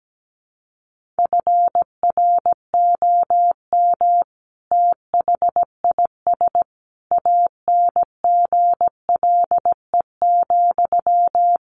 Morse code, heard as a keyed tone, about 17 wpm, 700 Hz. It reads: FROM THIS ANGLE,
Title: Sentences from Top 1500 Words 17wpm